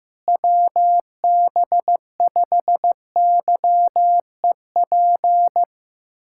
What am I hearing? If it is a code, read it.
Morse code, heard as a keyed tone, about 15 wpm, 700 Hz. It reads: WB5YEP
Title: USA Callsign DE Callsign - 6 Characters - 15wpm